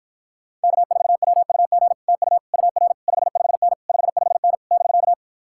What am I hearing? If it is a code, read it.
Morse code, heard as a keyed tone, about 46 wpm, 700 Hz. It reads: K4WUG TU UR 55N 55N <BK>